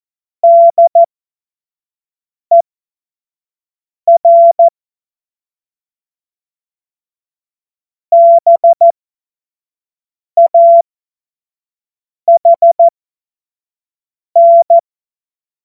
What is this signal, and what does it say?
Morse code, heard as a keyed tone, about 14 wpm, spaced as for 5 wpm, 680 Hz. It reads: DER BAHN
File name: Morsekurs / Witze / Witz 25. April 2023 25 F70